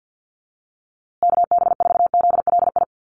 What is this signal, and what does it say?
Morse code, heard as a keyed tone, about 50 wpm, 700 Hz. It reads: KB4ZLI
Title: USA Call Signs Repeated 3x 50wpm